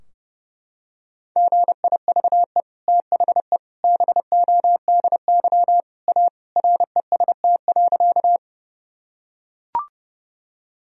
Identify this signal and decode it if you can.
Morse code, heard as a keyed tone, about 30 wpm, 700 Hz. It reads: GIVE THE BODY A REST.
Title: Sentences from Top 1000 Words - Review Part 2 - 30wpm